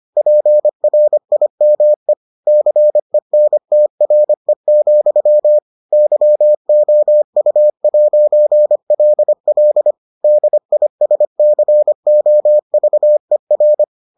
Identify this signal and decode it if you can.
Morse code, heard as a keyed tone, about 25 wpm, 600 Hz. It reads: PRIME CENTRE, YOU'LL DISCOVER